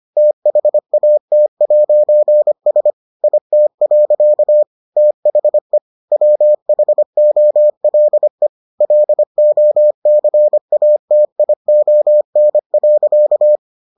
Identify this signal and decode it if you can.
Morse code, heard as a keyed tone, about 25 wpm, 600 Hz. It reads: THAT'S IT. THE WHOLE LOCATION.